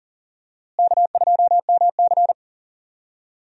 Morse code, heard as a keyed tone, about 40 wpm, 700 Hz. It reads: K2MC
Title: USA Call Signs - 4 Characters - Repeated 2x - 40wpm